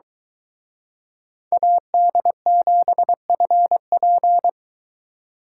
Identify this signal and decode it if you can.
Morse code, heard as a keyed tone, about 23 wpm, 700 Hz. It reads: AD7FP